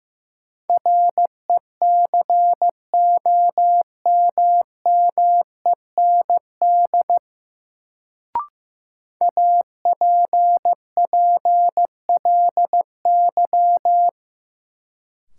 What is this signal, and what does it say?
Morse code, heard as a keyed tone, about 15 wpm, 700 Hz. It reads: RECOMMEND APPLY
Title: Sentences from Top 1200 Words - New Word Drill Alt - 15wpm